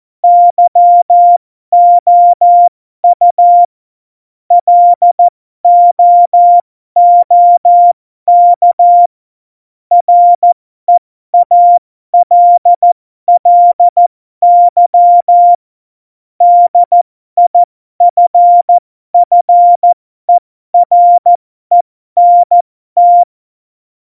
Morse code, heard as a keyed tone, about 14 wpm, spaced as for 12 wpm, 700 Hz. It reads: YOU LOOK REALLY DIFFERENT
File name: Morsekurs / Witze